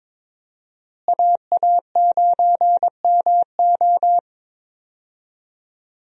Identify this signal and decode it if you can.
Morse code, heard as a keyed tone, about 21 wpm, 700 Hz. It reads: AA9MO
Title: USA Call Signs - 5 Characters - Repeated 2x - 22wpm